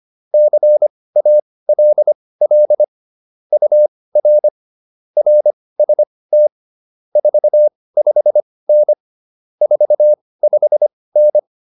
Morse code, heard as a keyed tone, about 25 wpm, spaced as for 18 wpm, 600 Hz. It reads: CALL UR RST 45N 45N
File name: QSO104 18